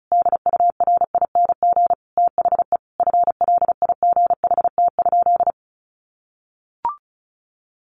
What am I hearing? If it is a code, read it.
Morse code, heard as a keyed tone, about 34 wpm, 700 Hz. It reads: DURING THE FLIGHT?
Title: Sentences from Top 1000 Words 35wpm